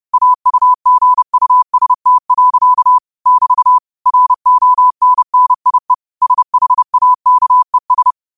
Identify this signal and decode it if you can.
Morse code, heard as a keyed tone, about 30 wpm, 1000 Hz. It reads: AUGUST. <BT> RONNIE SHAKES
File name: qotdfiles / 30wpm